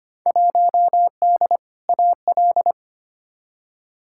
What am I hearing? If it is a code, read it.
Morse code, heard as a keyed tone, about 25 wpm, 700 Hz. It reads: 1D AL
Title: Field Day Exchanges - 25wpm